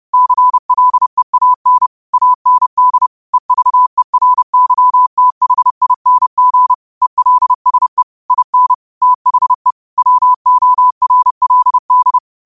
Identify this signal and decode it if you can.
Morse code, heard as a keyed tone, about 30 wpm, 1000 Hz. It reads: CLEAN AND EVERYTHING ELSE IN THE WORLD
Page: New quotes every day in morse code at 30 Words per minute.